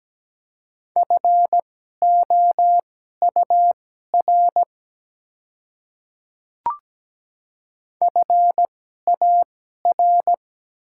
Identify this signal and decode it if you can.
Morse code, heard as a keyed tone, about 17 wpm, spaced as for 12 wpm, 700 Hz. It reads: FOUR FAR
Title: 2x Character Spacing - Top 300 Words - 17wpm